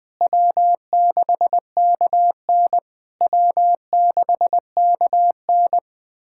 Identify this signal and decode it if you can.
Morse code, heard as a keyed tone, about 20 wpm, 690 Hz. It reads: W6KN W6KN
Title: SOTA QSO Practice 20wpm